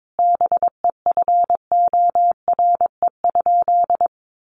Morse code, heard as a keyed tone, about 22 wpm, 700 Hz. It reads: BEFORE?
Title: Sentences from Top 900 Words 22wpm